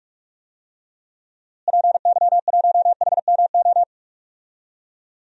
Morse code, heard as a keyed tone, about 46 wpm, 700 Hz. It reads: PY1HMO
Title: DXCC-100 Call Signs - 6 Characters - Repeated 2x - 45wpm